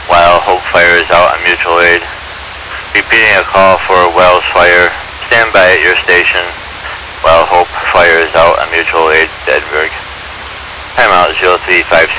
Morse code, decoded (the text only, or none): none